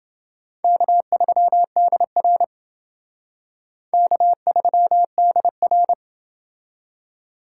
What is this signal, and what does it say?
Morse code, heard as a keyed tone, about 28 wpm, 700 Hz. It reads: K3DR K3DR